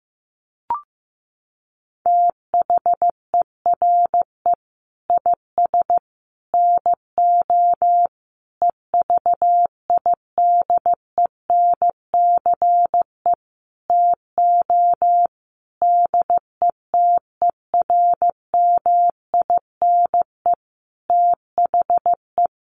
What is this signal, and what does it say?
Morse code, heard as a keyed tone, about 15 wpm, 700 Hz. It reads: THERE IS NO EVIDENCE TO DETERMINE THE
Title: Sentences from Top 1300 Words 15wpm